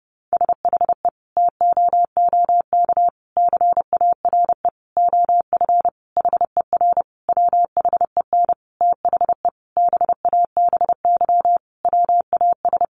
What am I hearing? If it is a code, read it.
Morse code, heard as a keyed tone, about 30 wpm, 700 Hz. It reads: SHE TOOK CARE OF HER WHEN THE BABY WAS
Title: Sentences from Top 1300 Words 30wpm